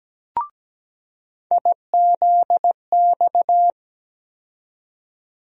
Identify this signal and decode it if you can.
Morse code, heard as a keyed tone, about 17 wpm, 700 Hz. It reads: IZX
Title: USA Call Sign Postfixes - 3 Letters 17wpm